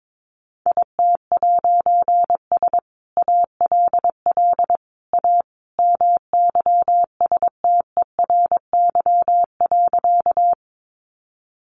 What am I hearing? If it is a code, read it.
Morse code, heard as a keyed tone, about 22 wpm, 700 Hz. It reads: IT'S ALL A MYSTERY.
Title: A Sound Of Thunder 22wpm